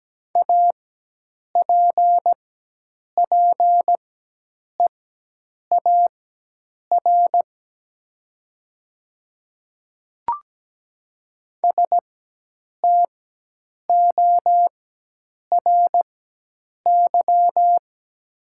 Morse code, heard as a keyed tone, about 17 wpm, spaced as for 8 wpm, 700 Hz. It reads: APPEAR STORY
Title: 4x Character Spacing - Top 500 Words - 17wpm